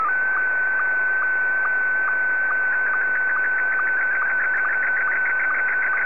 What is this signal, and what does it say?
SSTV (slow-scan television), Scottie 1.